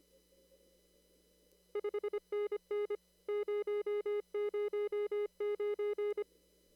Morse code, heard as a keyed tone, about 25 wpm, 410 Hz. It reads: 5NN 009
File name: CW / kt5j wpx cw-op-k5pi